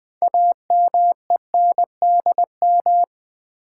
Morse code, heard as a keyed tone, about 20 wpm, 700 Hz. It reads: AMENDM
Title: Build A Word - 3 Letters At A Time - 9-Letter Words - 20wpm